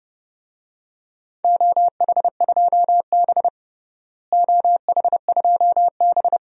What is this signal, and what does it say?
Morse code, heard as a keyed tone, about 32 wpm, 700 Hz. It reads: OH2B OH2B
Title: DXCC-100 Call Signs - 4 Characters - Repeated 3x - 30wpm